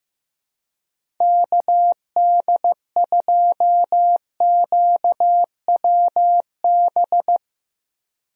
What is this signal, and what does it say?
Morse code, heard as a keyed tone, about 16 wpm, 700 Hz. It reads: KD2QWB
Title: USA Call Signs - 6 Characters - Repeated 3x - 15wpm